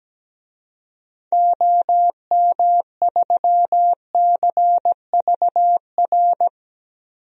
Morse code, heard as a keyed tone, about 17 wpm, 700 Hz. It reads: OM3CVR
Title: DXCC-100 Call Signs - 6 Characters - 17wpm